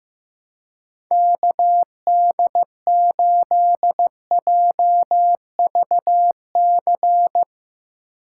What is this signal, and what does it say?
Morse code, heard as a keyed tone, about 16 wpm, 700 Hz. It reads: KD8JVC
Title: Speed Racing - USA Call Signs 15wpm